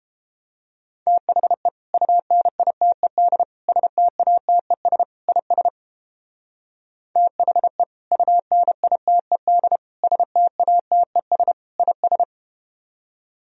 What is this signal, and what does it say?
Morse code, heard as a keyed tone, about 31 wpm, 700 Hz. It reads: THE UNITED STATES IS THE UNITED STATES IS